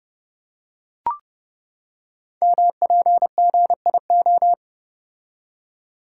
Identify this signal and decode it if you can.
Morse code, heard as a keyed tone, about 30 wpm, 700 Hz. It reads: MPGIO
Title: Code Groups - 5 Letters - 30wpm